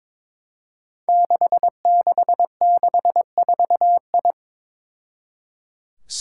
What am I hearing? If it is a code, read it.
Morse code, heard as a keyed tone, about 23 wpm, 700 Hz. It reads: 6664I
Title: Five Challenging Letter-Number Combinations 22wpm